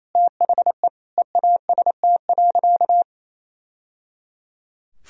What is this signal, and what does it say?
Morse code, heard as a keyed tone, about 28 wpm, 700 Hz. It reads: THE EAST.
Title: Sentences from Top 1000 Words - Review Part 3 - 28wpm